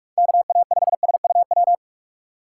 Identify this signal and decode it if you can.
Morse code, heard as a keyed tone, about 46 wpm, 700 Hz. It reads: KA5SUW
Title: USA Call Signs - 6 Characters - 2x Character Spacing - 45wpm